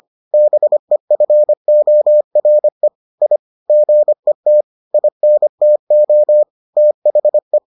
Morse code, heard as a keyed tone, about 25 wpm, 600 Hz. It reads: BEFORE I GET INTO THE